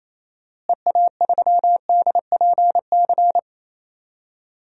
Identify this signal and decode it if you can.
Morse code, heard as a keyed tone, about 28 wpm, 700 Hz. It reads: EA3DPC
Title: DXCC-100 Call Signs - 6 Characters - Repeated 3x - 28wpm